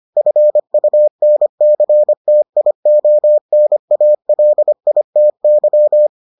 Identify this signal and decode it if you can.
Morse code, heard as a keyed tone, about 25 wpm, 600 Hz. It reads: FUNCTIONALITY